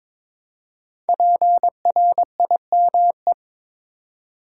Morse code, heard as a keyed tone, about 23 wpm, 690 Hz. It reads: PRIME
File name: 5 Letter Words - First 3 Letters Given - 22wpm